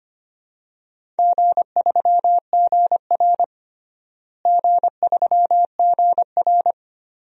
Morse code, heard as a keyed tone, about 25 wpm, 700 Hz. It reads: G3GR G3GR
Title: DXCC-100 Call Signs - 4 Characters - Repeated 2x - 25wpm